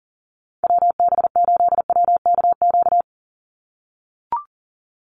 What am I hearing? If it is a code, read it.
Morse code, heard as a keyed tone, about 39 wpm, 700 Hz. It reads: WB8WKQ
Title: Call Signs - 40wpm